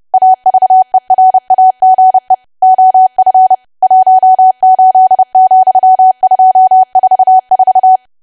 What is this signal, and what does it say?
Morse code, heard as a keyed tone, about 30 wpm, 750 Hz. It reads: AVERAGE OF 18,244
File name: cwa / W1AW-Level3 / 30WPM